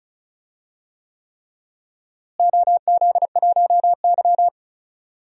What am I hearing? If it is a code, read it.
Morse code, heard as a keyed tone, about 36 wpm, 690 Hz. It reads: OZ1Y
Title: DXCC-100 Call Signs - 4 Characters - Repeated 3x - 35wpm